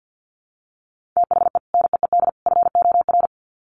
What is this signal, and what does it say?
Morse code, heard as a keyed tone, about 50 wpm, 700 Hz. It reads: THE NEED FOR
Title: Sets of 3 Words 50wpm